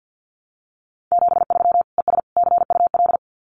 Morse code, heard as a keyed tone, about 50 wpm, 700 Hz. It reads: 73 ES CUL
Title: Common QSO Phrases Encore 50wpm